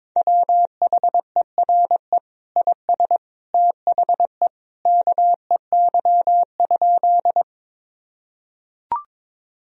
Morse code, heard as a keyed tone, about 22 wpm, 700 Hz. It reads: WHERE IS THE KEY?
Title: Sentences from Top 1000 Words - Review Part 4 - 22wpm